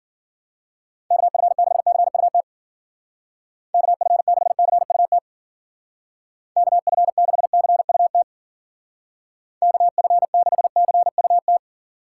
Morse code, heard as a keyed tone, about 49 wpm, 700 Hz. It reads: KF6CUT KF6CUT KF6CUT KF6CUT
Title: Speed Racing - USA 2x3 Call Signs 40wpm